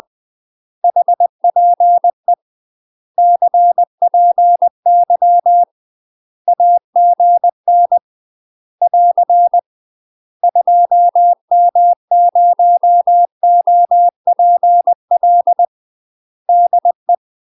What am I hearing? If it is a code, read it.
Morse code, heard as a keyed tone, about 20 wpm, 700 Hz. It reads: HPE CPY AGN <AR> 2M0OPL DE
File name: UK MASTER QSOs TRAINING / 20WPM / UK MASTER QSOs 0052